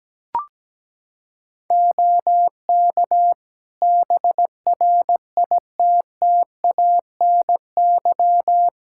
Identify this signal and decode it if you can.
Morse code, heard as a keyed tone, about 17 wpm, 700 Hz. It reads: OK BRITTANY
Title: K1USN SST QSO Practice 17wpm